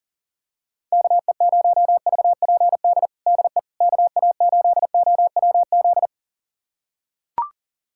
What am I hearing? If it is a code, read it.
Morse code, heard as a keyed tone, about 40 wpm, 690 Hz. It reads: KE0VPD DE KA8OWZ
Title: USA Callsign DE Callsign - 6 Characters - 40wpm